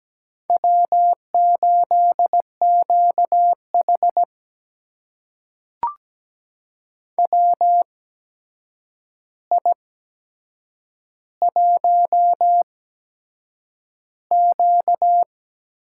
Morse code, heard as a keyed tone, about 17 wpm, spaced as for 7 wpm, 700 Hz. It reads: W8QH WI1Q
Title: USA Call Signs - 4 Characters - 8x Character Spacing - 17wpm